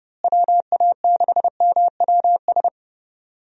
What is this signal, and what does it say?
Morse code, heard as a keyed tone, about 30 wpm, 700 Hz. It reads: WA6MWS
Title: USA Call Signs - 6 Characters - Repeated 3x - 30wpm